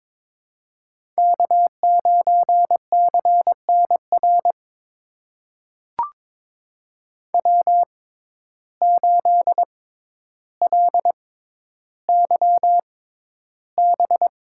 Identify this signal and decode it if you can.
Morse code, heard as a keyed tone, about 22 wpm, spaced as for 11 wpm, 700 Hz. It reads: K9CNR W8LYB
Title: USA Call Signs - 5 Characters - 6x Character Spacing - 22wpm